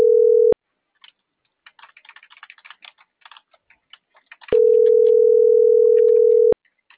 Ringback tone (North America), 440 + 480 Hz.